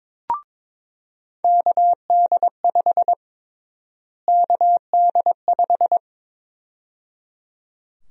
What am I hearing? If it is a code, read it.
Morse code, heard as a keyed tone, about 22 wpm, 700 Hz. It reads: KD5 KD5
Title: USA Call Sign Prefixes - Repeated 2x - 22wpm